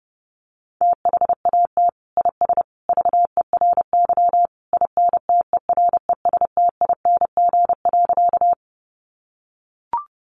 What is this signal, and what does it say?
Morse code, heard as a keyed tone, about 30 wpm, 700 Hz. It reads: THAT IS VERY INTERESTING.
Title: Sentences from Top 1000 Words 30wpm